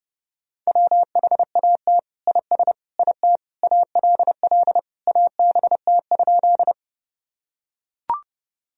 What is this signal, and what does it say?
Morse code, heard as a keyed tone, about 30 wpm, 700 Hz. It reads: WHAT IS IT ALL ABT?
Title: Sentences from Top 100 Words and Abbreviations 30wpm